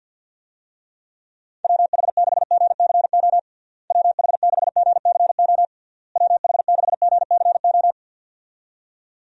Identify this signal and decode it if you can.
Morse code, heard as a keyed tone, about 50 wpm, 700 Hz. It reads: WH6GCO WH6GCO WH6GCO
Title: USA Call Signs - 6 Characters - Repeated 3x - 50wpm